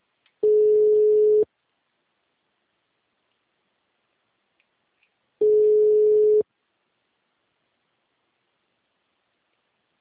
Ringback tone (Europe, 425 Hz).